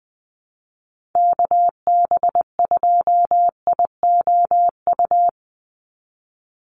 Morse code, heard as a keyed tone, about 20 wpm, 700 Hz. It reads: KB2IOU
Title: USA Call Signs 20wpm